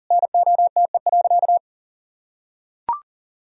Morse code, heard as a keyed tone, about 40 wpm, 700 Hz. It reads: NOTE.
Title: Sentences from Top 1000 Words - Review Part 2 - 40wpm